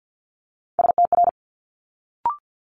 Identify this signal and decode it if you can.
Morse code, heard as a keyed tone, about 50 wpm, 700 Hz. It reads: STR
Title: Multi-letter Phonemes Quick Fire 50wpm